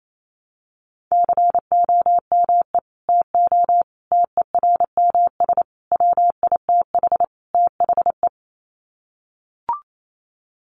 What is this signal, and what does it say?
Morse code, heard as a keyed tone, about 28 wpm, 700 Hz. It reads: COME TO TERMS WITH THE
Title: Sets of 5 Words Encore2 28wpm